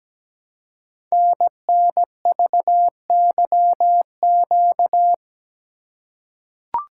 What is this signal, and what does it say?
Morse code, heard as a keyed tone, about 17 wpm, 700 Hz. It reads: NNVYQ
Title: Code Groups - 5 Letters - 17wpm